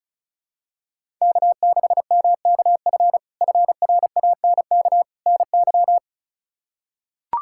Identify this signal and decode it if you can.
Morse code, heard as a keyed tone, about 36 wpm, 700 Hz. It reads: K6MKF FRANK NY
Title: K1USN SST QSO Practice - 35wpm